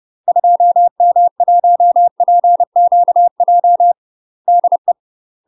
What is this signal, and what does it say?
Morse code, heard as a keyed tone, about 30 wpm, 700 Hz. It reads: 2M1PQJ DE
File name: UK MASTER QSOs TRAINING / 30WPM / UK MASTER QSOs 0038